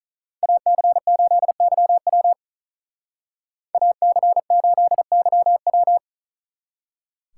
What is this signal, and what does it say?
Morse code, heard as a keyed tone, about 45 wpm, 700 Hz. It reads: AC8YW AC8YW